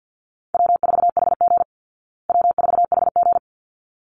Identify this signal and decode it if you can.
Morse code, heard as a keyed tone, about 50 wpm, 700 Hz. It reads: W4HG W4HG
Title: Easy Call Signs Repeated 2x - 50wpm